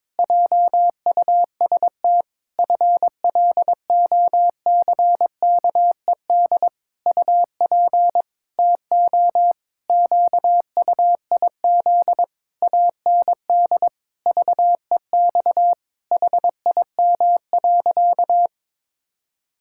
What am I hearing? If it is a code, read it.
Morse code, heard as a keyed tone, about 22 wpm, 700 Hz. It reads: JUST FLOCKED UP TO QUIZ AND VEX HIM.
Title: Pangram Challenge v2 - 22wpm